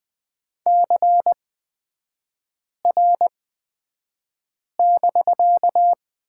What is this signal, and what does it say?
Morse code, heard as a keyed tone, about 20 wpm, spaced as for 5 wpm, 700 Hz. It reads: CR<BK>
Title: 250 Lesson 40e - BK - Three Characters 20wpm